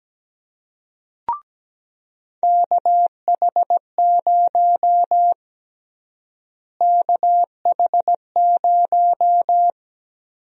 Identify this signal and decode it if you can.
Morse code, heard as a keyed tone, about 17 wpm, 700 Hz. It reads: KH0 KH0
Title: USA Call Sign Prefixes - Repeated 2x - 17wpm